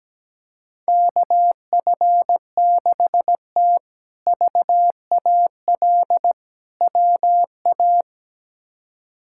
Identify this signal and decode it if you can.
Morse code, heard as a keyed tone, about 17 wpm, 700 Hz. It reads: KF6T VAL WA
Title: K1USN SST QSO Practice - 17wpm